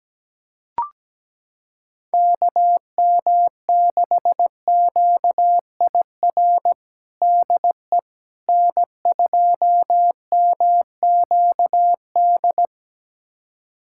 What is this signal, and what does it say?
Morse code, heard as a keyed tone, about 17 wpm, 700 Hz. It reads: KM6QIR DE N2MQD
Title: USA Callsign DE Callsign 17wpm